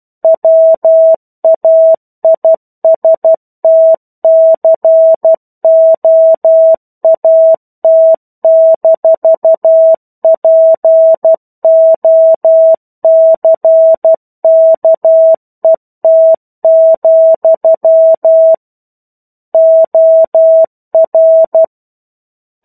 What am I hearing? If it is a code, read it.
Morse code, heard as a keyed tone, about 12 wpm, 650 Hz. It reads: WAISTCOAT-POCKET, OR
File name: 01-Chapter-01 12 WPM